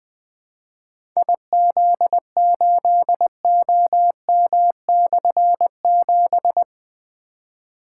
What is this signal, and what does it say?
Morse code, heard as a keyed tone, about 20 wpm, 700 Hz. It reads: IZ8OM/7
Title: Difficult Call Signs 20wpm